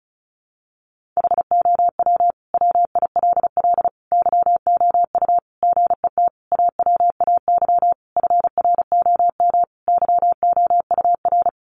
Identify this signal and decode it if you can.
Morse code, heard as a keyed tone, about 34 wpm, 700 Hz. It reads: HOW WILL YOU GET AWAY FROM YOUR
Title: Sentences from Top 1500 Words 35wpm